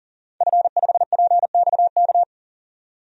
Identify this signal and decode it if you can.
Morse code, heard as a keyed tone, about 40 wpm, 700 Hz. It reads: F5PXK
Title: DXCC-100 Call Signs - 5 Characters - 40wpm